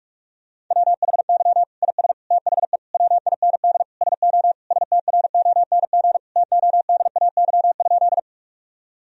Morse code, heard as a keyed tone, about 46 wpm, 700 Hz. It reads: WHY IS THE WIND SO STRONG TODAY?
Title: Sentences from Top 1000 Words - Review Part 2 - 45wpm